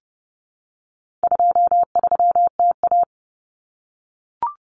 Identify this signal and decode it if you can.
Morse code, heard as a keyed tone, about 29 wpm, 700 Hz. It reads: 23TA
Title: Common Cut Numbers - 4-Digit 30wpm